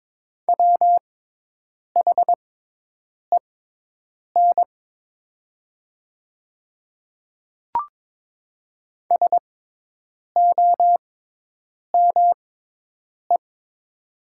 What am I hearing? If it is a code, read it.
Morse code, heard as a keyed tone, about 22 wpm, spaced as for 8 wpm, 700 Hz. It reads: WHEN SOME
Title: 6x Character Spacing - Top 200 Words - 22wpm